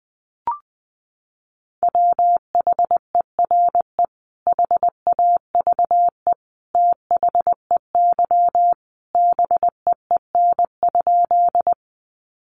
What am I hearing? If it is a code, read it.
Morse code, heard as a keyed tone, about 20 wpm, 700 Hz. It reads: WHERE HAVE THEY BEEN?
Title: Sentences from Top 200 Words 20wpm